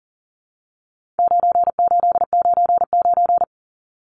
Morse code, heard as a keyed tone, about 40 wpm, 700 Hz. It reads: 9899
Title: Numbers - 4-Digit 40wpm